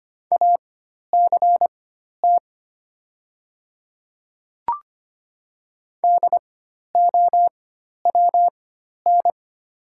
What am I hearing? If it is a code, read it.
Morse code, heard as a keyed tone, about 25 wpm, spaced as for 12 wpm, 700 Hz. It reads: ACT DOWN
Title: 4x Character Spacing - Top 200 Words - 25wpm